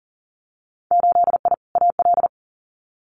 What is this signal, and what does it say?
Morse code, heard as a keyed tone, about 37 wpm, 700 Hz. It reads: 8I AL
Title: Winter Field Day Exchanges Encore 40wpm